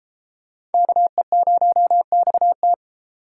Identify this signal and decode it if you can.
Morse code, heard as a keyed tone, about 33 wpm, 700 Hz. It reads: KE0XT